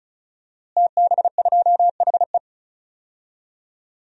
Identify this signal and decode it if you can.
Morse code, heard as a keyed tone, about 34 wpm, 700 Hz. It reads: TB2HE
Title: DXCC-100 Call Signs - 5 Characters - Repeated 3x - 35wpm